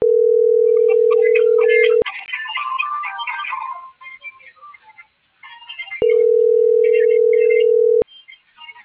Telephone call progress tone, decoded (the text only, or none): ringback tone